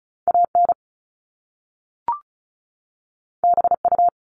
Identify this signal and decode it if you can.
Morse code, heard as a keyed tone, about 36 wpm, 700 Hz. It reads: AN BU
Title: Top Two Letter Combinations in Words 35wpm